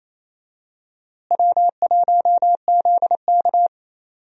Morse code, heard as a keyed tone, about 28 wpm, 700 Hz. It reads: W1ZK
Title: USA Call Signs - 4 Characters - Repeated 3x - 28wpm